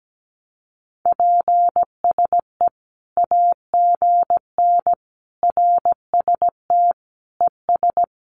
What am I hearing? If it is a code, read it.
Morse code, heard as a keyed tone, about 17 wpm, 700 Hz. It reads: PSE AGN RST ES
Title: Common QSO Phrases 17wpm